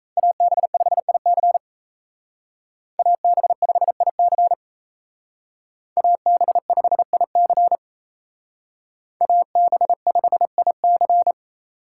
Morse code, heard as a keyed tone, about 36 wpm, 700 Hz. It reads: AB5IC AB5IC AB5IC AB5IC